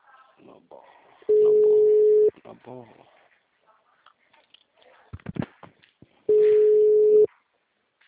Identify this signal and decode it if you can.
Ringback tone (Europe, 425 Hz), with speech.
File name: out- -2003- - - .14175